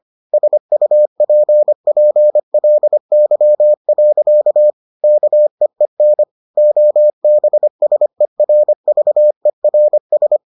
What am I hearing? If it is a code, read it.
Morse code, heard as a keyed tone, about 25 wpm, 600 Hz. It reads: SUPPLY. KEEN OBSERVERS